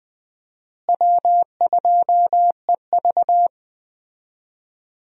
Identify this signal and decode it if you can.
Morse code, heard as a keyed tone, about 20 wpm, 700 Hz. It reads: W2EV
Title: USA Call Signs - 4 Characters - Repeated 2x - 20wpm